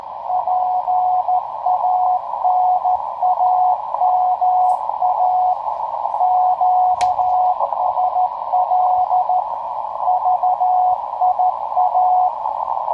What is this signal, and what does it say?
Morse code, heard as a keyed tone, about 14 wpm, 730 Hz, with radio noise. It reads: PANAMA QSL VIA
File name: tracks / hp1rcp